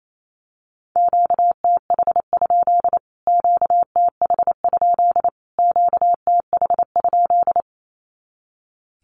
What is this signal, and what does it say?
Morse code, heard as a keyed tone, about 28 wpm, 700 Hz. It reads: QTH? QTH? QTH?
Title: Q-Codes Repeated 3x 28wpm